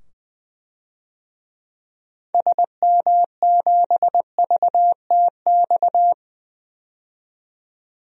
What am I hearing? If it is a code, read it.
Morse code, heard as a keyed tone, about 20 wpm, 700 Hz. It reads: SM7VTX
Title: DXCC-100 Call Signs - 6 Characters - 20wpm